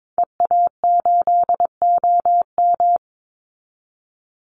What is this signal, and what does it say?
Morse code, heard as a keyed tone, about 22 wpm, 700 Hz. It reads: EA8OM
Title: Call Signs Repeated 2x - 22wpm